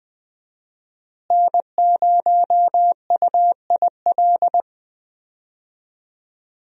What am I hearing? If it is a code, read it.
Morse code, heard as a keyed tone, about 20 wpm, 700 Hz. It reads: N0UIL
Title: USA Call Signs - 5 Characters - Repeated 2x - 20wpm